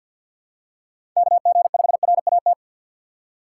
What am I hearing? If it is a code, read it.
Morse code, heard as a keyed tone, about 48 wpm, 700 Hz. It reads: KG5RAT
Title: USA Call Signs - 6 Characters - 8x Character Spacing - 50wpm